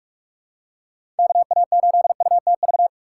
Morse code, heard as a keyed tone, about 46 wpm, 700 Hz. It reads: KA8UTV
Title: USA Call Signs - 2x Character Spacing - 45wpm